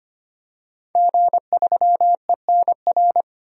Morse code, heard as a keyed tone, about 25 wpm, 700 Hz. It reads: G3ENR
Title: DXCC-100 Call Signs - 5 Characters - 25wpm